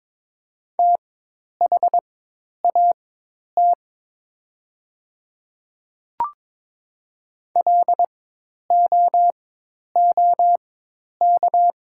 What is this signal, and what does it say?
Morse code, heard as a keyed tone, about 22 wpm, spaced as for 10 wpm, 700 Hz. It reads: THAT LOOK
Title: 4x Character Spacing - Top 100 Words - 22wpm